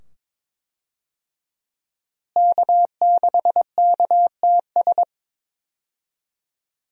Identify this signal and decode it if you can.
Morse code, heard as a keyed tone, about 22 wpm, 700 Hz. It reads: K6KTS
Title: USA Call Signs - 5 Characters - Repeated 2x - 22wpm